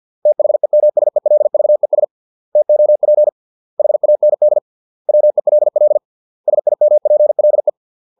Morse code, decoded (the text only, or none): THEMSELVES TOP HAND WELL SIMPLE